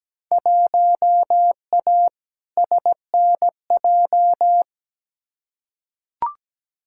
Morse code, heard as a keyed tone, about 17 wpm, 700 Hz. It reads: 1A SNJ
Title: Field Day Exchanges - 17wpm